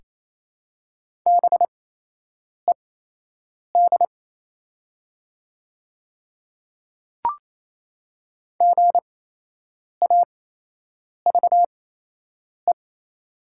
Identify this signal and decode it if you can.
Morse code, heard as a keyed tone, about 28 wpm, spaced as for 8 wpm, 700 Hz. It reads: BED GAVE
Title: 8x Character Spacing - Top 500 Words - 28wpm